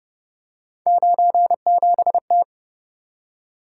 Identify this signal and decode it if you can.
Morse code, heard as a keyed tone, about 30 wpm, 700 Hz. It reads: 97T